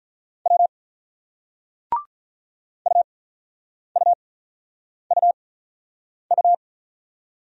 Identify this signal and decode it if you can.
Morse code, heard as a keyed tone, about 44 wpm, spaced as for 9 wpm, 700 Hz. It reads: W UUUU